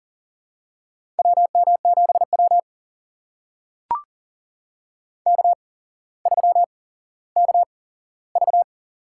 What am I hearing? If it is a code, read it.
Morse code, heard as a keyed tone, about 40 wpm, spaced as for 17 wpm, 700 Hz. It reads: WM7W K3KV
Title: USA Call Signs - 4 Characters - 8x Character Spacing - 40wpm